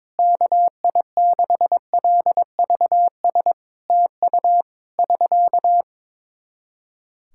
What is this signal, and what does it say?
Morse code, heard as a keyed tone, about 23 wpm, 700 Hz. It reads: KI6LVS TU <SK>
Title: Practice QSO - Exchange 4 - Ending - 22wpm